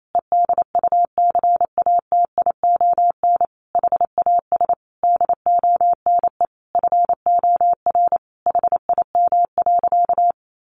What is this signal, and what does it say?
Morse code, heard as a keyed tone, about 28 wpm, 700 Hz. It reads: EDUCATION HAS DONE FOR HIM.
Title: Sentences from Top 1400 Words 28wpm